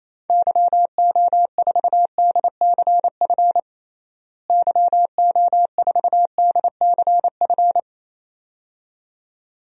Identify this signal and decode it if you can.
Morse code, heard as a keyed tone, about 27 wpm, 700 Hz. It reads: YO4DCF YO4DCF
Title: DXCC-100 Call Signs - 6 Characters - Repeated 2x - 28wpm